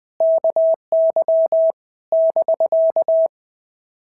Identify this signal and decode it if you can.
Morse code, heard as a keyed tone, about 20 wpm, 650 Hz. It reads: KY <BK>
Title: POTA QSO Practice - 20wpm